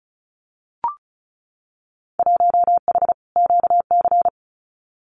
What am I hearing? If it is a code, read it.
Morse code, heard as a keyed tone, about 36 wpm, 690 Hz. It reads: 1H QC
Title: Winter Field Day Exchanges 35wpm